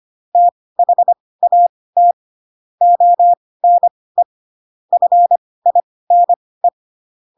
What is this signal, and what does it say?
Morse code, heard as a keyed tone, about 25 wpm, spaced as for 18 wpm, 700 Hz. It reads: THAT ONE FINE
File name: Pinocchio / Pinocchio ch1 12 25 ik1hge